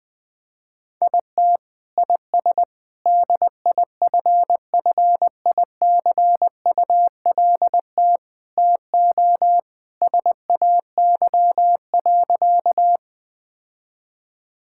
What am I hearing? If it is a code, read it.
Morse code, heard as a keyed tone, about 20 wpm, 700 Hz. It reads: IT IS DIFFICULT TO SAY.
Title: Sentences with 4-Word Phrases - 20wpm